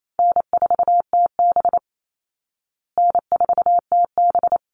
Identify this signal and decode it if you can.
Morse code, heard as a keyed tone, about 28 wpm, 700 Hz. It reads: N4TB N4TB
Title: Easy Call Signs Repeated 3x - 28wpm